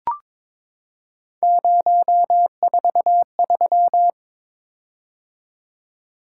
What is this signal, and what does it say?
Morse code, heard as a keyed tone, about 22 wpm, 700 Hz. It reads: 043
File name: practice / 22wpm / Three Letter-Number Combinations - 22wpm